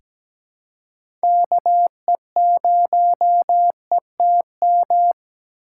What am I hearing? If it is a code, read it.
Morse code, heard as a keyed tone, about 17 wpm, 700 Hz. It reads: KE0ETM
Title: USA Call Signs - 6 Characters - 6x Character Spacing - 17wpm